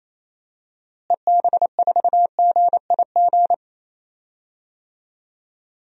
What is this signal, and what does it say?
Morse code, heard as a keyed tone, about 28 wpm, 700 Hz. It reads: EB4GIG
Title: DXCC-100 Call Signs - 6 Characters - Repeated 2x - 28wpm